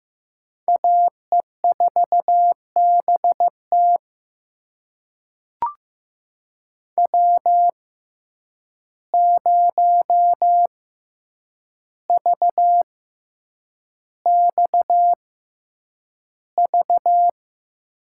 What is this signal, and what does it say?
Morse code, heard as a keyed tone, about 15 wpm, spaced as for 8 wpm, 700 Hz. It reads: AE4BT W0VXV
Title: USA Call Signs - 5 Characters - 6x Character Spacing - 15wpm